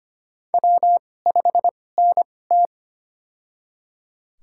Morse code, heard as a keyed tone, about 25 wpm, spaced as for 18 wpm, 700 Hz. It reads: W5NT
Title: USA Call Signs - 4 Characters - 2x Character Spacing - 25wpm